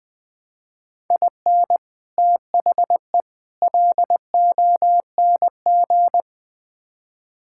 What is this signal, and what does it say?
Morse code, heard as a keyed tone, about 20 wpm, 700 Hz. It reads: IN THE LONG
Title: Build A Phrase - 4-Word Phrases - 20wpm